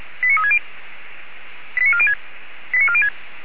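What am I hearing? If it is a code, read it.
Five-tone selective call (ZVEI), heard as quick runs of five tones, tones of about 70 ms. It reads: 89360 89406 89406
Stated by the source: ZVEI five-tone selective call 89233 received by SDR, logged by Feuerwehr Orlamünde.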